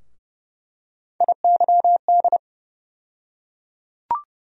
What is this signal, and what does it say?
Morse code, heard as a keyed tone, about 31 wpm, 700 Hz. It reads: IYD
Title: USA Call Sign Postfixes - 3 Letters 30wpm